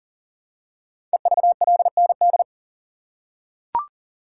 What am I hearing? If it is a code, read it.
Morse code, heard as a keyed tone, about 42 wpm, 700 Hz. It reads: EVLND
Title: Code Groups - 5 Letter:Numbers - 40wpm